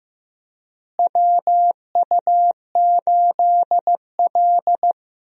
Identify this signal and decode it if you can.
Morse code, heard as a keyed tone, about 15 wpm, 700 Hz. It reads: WU8L
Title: USA Call Signs - 4 Characters - Repeated 3x - 15wpm